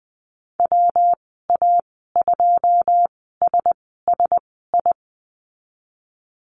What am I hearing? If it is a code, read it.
Morse code, heard as a keyed tone, about 20 wpm, spaced as for 15 wpm, 700 Hz. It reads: WA2SSI
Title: 161 Lesson 26g - Letter g - Call Signs 20wpm